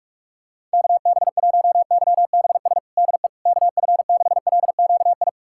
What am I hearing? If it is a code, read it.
Morse code, heard as a keyed tone, about 47 wpm, 700 Hz. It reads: KB1YBS DE KF6LQI
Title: USA Callsign DE Callsign - 6 Characters - 45wpm